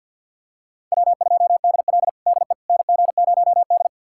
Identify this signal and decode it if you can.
Morse code, heard as a keyed tone, about 52 wpm, 700 Hz. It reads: W2DL DE NG0D
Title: USA Callsign DE Callsign - 4 Characters - 50wpm